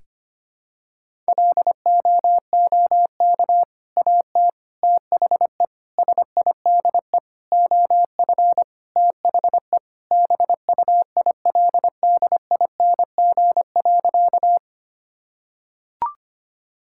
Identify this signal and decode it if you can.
Morse code, heard as a keyed tone, about 25 wpm, 700 Hz. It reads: LOOK AT THE SIDE OF THE BUILDING.
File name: practice / 25wpm / Sentences with 3-Word Phrases - 25wpm